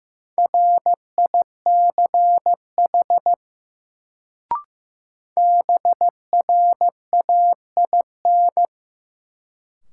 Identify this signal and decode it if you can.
Morse code, heard as a keyed tone, about 15 wpm, 700 Hz. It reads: RICH BRAIN
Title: Sentences from Top 1100 Words - New Word Drill Alt - 15wpm